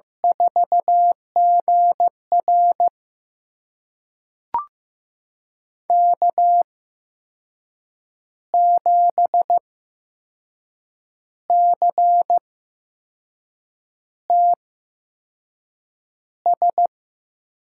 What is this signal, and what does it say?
Morse code, heard as a keyed tone, about 15 wpm, spaced as for 5 wpm, 700 Hz. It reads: 4GR K7CTS
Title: USA Call Signs - 5 Characters - 8x Character Spacing - 15wpm